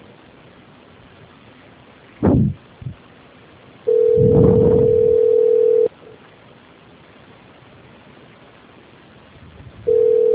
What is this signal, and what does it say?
Ringback tone (North America), 440 + 480 Hz.
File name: out- -2033- - - .44